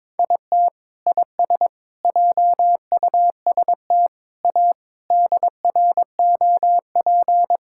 Morse code, heard as a keyed tone, about 22 wpm, 700 Hz. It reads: IT IS JUST A DROP
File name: practice / 22wpm / Sentences with 8-Word Phrases - 22wpm